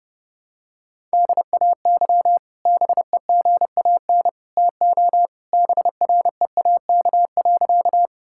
Morse code, heard as a keyed tone, about 32 wpm, 700 Hz. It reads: DAY BEGAN TO BREAK.
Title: Sentences from Top 1000 Words - Review Part 6 - 30wpm